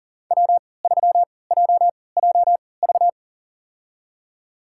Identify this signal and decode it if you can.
Morse code, heard as a keyed tone, about 38 wpm, spaced as for 22 wpm, 700 Hz. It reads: W3JJV
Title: USA Call Signs - 5 Characters - 3x Character Spacing - 40wpm